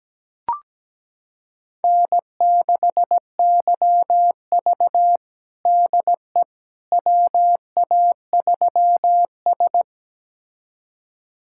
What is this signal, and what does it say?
Morse code, heard as a keyed tone, about 17 wpm, 700 Hz. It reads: N6YV DE WA3S
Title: USA Callsign DE Callsign - 4 Characters - 17wpm